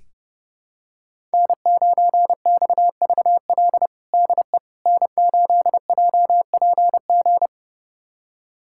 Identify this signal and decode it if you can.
Morse code, heard as a keyed tone, about 30 wpm, 700 Hz. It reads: N9XVL DE N8JPG
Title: USA Callsign DE Callsign - 5 Characters - 30wpm